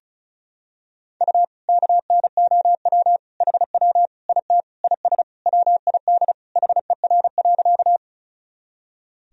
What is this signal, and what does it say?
Morse code, heard as a keyed tone, about 34 wpm, 700 Hz. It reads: U KNOW HW IT IS WID HER.
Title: Sentences from Top 100 Words and Abbreviations 35wpm